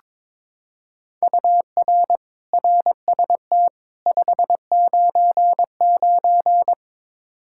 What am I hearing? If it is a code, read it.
Morse code, heard as a keyed tone, about 22 wpm, 700 Hz. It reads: UR RST 599
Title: Practice QSO - Exchange 1 - Beginning - 22wpm